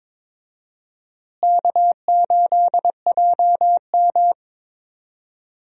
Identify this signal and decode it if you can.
Morse code, heard as a keyed tone, about 22 wpm, 700 Hz. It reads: K8JM
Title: USA Call Signs - 4 Characters - Repeated 3x - 22wpm